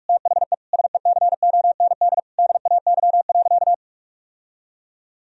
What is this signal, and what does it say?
Morse code, heard as a keyed tone, about 46 wpm, 700 Hz. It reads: THE SECOND DAY.
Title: Sentences from Top 1000 Words - Review Part 4 - 45wpm